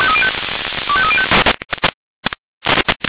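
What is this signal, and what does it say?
Five-tone selective call (ZVEI), heard as quick runs of five tones, tones of about 70 ms. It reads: 25305
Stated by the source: ZVEI five-tone selective call received by SDR.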